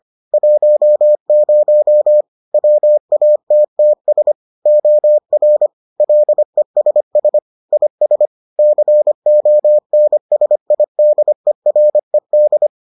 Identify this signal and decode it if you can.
Morse code, heard as a keyed tone, about 25 wpm, 600 Hz. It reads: 10 WATTS OR LESS IS CONSIDERED